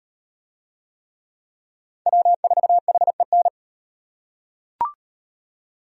Morse code, heard as a keyed tone, about 38 wpm, 700 Hz. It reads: W4HEN